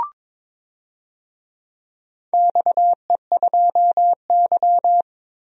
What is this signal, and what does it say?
Morse code, heard as a keyed tone, about 22 wpm, 700 Hz. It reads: XE2Y
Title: DXCC-100 Call Signs - 4 Characters - 22wpm